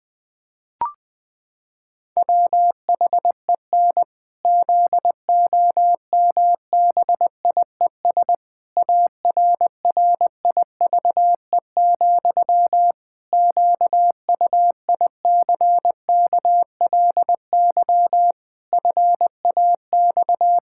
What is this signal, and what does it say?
Morse code, heard as a keyed tone, about 20 wpm, 700 Hz. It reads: WHEN ZOMBIES ARRIVE, QUICKLY FAX
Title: Pangram Challenge v2 - 20wpm